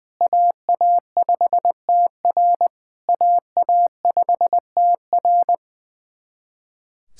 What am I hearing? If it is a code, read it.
Morse code, heard as a keyed tone, about 20 wpm, 700 Hz. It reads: AA5TR AA5TR
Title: SOTA QSO Practice 20wpm